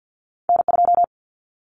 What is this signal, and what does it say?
Morse code, heard as a keyed tone, about 50 wpm, 700 Hz. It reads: N2
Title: Callsign Prefixes for Top 25 DXCC - 50wpm